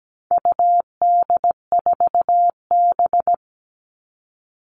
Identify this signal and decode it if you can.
Morse code, heard as a keyed tone, about 17 wpm, 700 Hz. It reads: UD4B
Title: All Cut Numbers - 4-Digit 17wpm